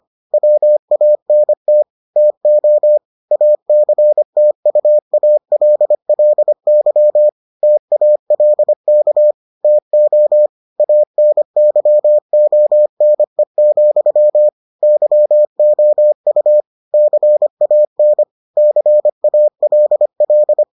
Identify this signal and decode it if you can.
Morse code, heard as a keyed tone, about 25 wpm, 600 Hz. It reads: WANT TO ACTUALLY TALK TO ANYONE, YOU CAN CALL